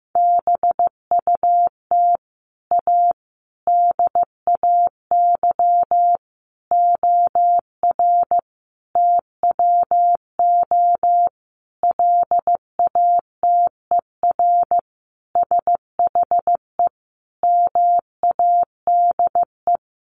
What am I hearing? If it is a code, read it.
Morse code, heard as a keyed tone, about 15 wpm, 700 Hz. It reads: BUT A DAY OR TWO LATER SHE MADE
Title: Sentences from Top 1100 Words 15wpm